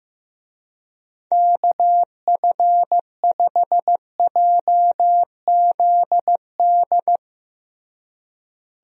Morse code, heard as a keyed tone, about 15 wpm, 700 Hz. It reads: KF5JZD
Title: USA Call Signs - 6 Characters - Repeated 2x - 15wpm